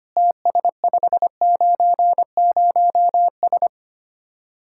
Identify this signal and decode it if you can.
Morse code, heard as a keyed tone, about 31 wpm, 700 Hz. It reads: TS590S
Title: Speed-Racing Common QSO Phrases Encore - 25wpm